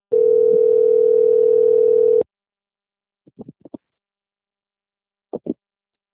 Ringback tone (North America), 440 + 480 Hz.
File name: out- - - - - .15570